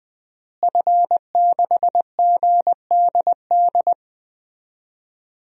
Morse code, heard as a keyed tone, about 20 wpm, 700 Hz. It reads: F6GDD
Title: DXCC-100 Call Signs - 5 Characters - Repeated 3x - 20wpm